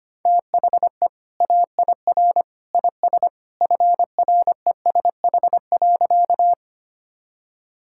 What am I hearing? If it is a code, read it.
Morse code, heard as a keyed tone, about 25 wpm, 700 Hz. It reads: THE AIR IS FRESH.
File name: practice / 25wpm / Sentences with 2-Word Phrases - 25wpm